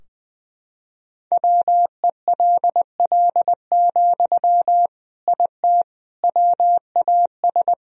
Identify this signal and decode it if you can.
Morse code, heard as a keyed tone, about 21 wpm, 700 Hz. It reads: WELL, IT WAS
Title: Sentences from Top 1000 Words - Review Part 3 - 20wpm